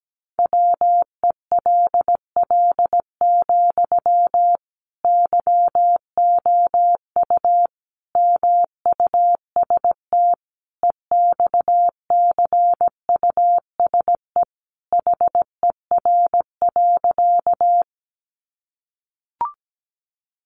Morse code, heard as a keyed tone, about 17 wpm, 700 Hz. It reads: WELL, YOU MUST EXCUSE HER.
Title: Sentences from Top 700 Words 17wpm